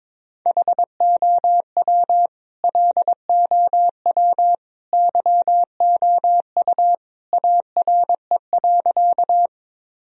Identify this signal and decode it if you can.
Morse code, heard as a keyed tone, about 22 wpm, 700 Hz. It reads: HOW LOW YOU ARE.
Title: Sentences from Top 1000 Words - Review Part 1 - 22wpm